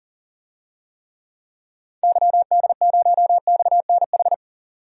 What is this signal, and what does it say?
Morse code, heard as a keyed tone, about 40 wpm, 700 Hz. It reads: YD0XNH
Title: DXCC-100 Call Signs - 6 Characters - 40wpm